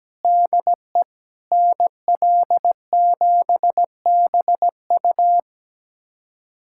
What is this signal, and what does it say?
Morse code, heard as a keyed tone, about 17 wpm, 700 Hz. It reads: DE NL7BU
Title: USA Callsign DE Callsign - 5 Characters - 17wpm